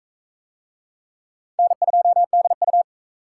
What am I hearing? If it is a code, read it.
Morse code, heard as a keyed tone, about 37 wpm, 700 Hz. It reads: N2DU